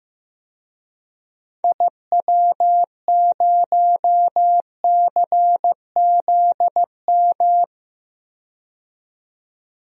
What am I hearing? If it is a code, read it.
Morse code, heard as a keyed tone, about 15 wpm, 700 Hz. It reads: IW0CZM
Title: DXCC-100 Call Signs - 6 Characters - 15wpm